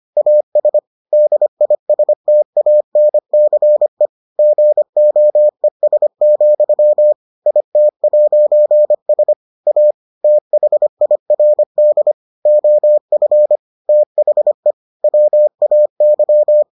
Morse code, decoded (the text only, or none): AS DISTANCE GOES, IT'S A THIRD OF THE WAY